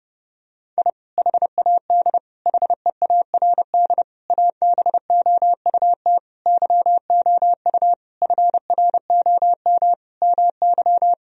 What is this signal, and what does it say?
Morse code, heard as a keyed tone, about 30 wpm, 700 Hz. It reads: I HAD HEARD ABOUT YOU FROM MY
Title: Sentences from Top 1000 Words - Review Part 1 - 30wpm